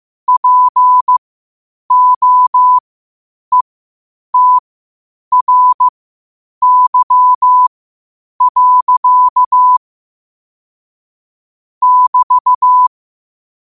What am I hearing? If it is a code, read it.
Morse code, heard as a keyed tone, about 15 wpm, spaced as for 8 wpm, 1000 Hz. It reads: POETRY. <BT>
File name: qotdfiles / 8wpm